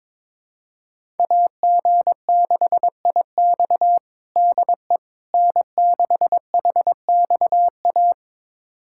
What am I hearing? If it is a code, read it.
Morse code, heard as a keyed tone, about 22 wpm, 700 Hz. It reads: AG6IX DE N6HXA
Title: USA Callsign DE Callsign - 5 Characters - 22wpm